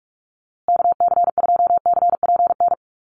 Morse code, heard as a keyed tone, about 44 wpm, 700 Hz. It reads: KC2CPN
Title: USA Call Signs Repeated 3x 45wpm